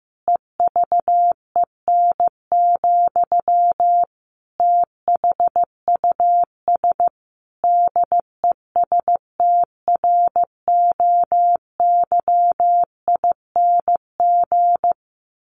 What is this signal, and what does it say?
Morse code, heard as a keyed tone, about 15 wpm, 700 Hz. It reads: EVEN, THUS DESTROYING
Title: A Sound Of Thunder 15wpm